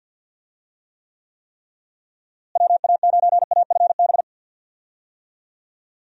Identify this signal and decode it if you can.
Morse code, heard as a keyed tone, about 52 wpm, 700 Hz. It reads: WA9AFB
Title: USA Call Signs - 6 Characters - Repeated 3x - 50wpm